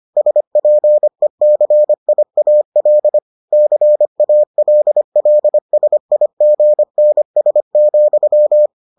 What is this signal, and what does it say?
Morse code, heard as a keyed tone, about 25 wpm, 600 Hz. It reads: SPECIAL CALLSIGNS,